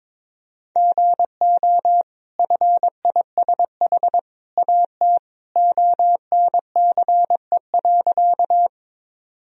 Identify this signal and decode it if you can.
Morse code, heard as a keyed tone, about 22 wpm, 700 Hz. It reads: GO FISH AT ONCE.
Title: Sentences from Top 1000 Words - Review Part 1 - 22wpm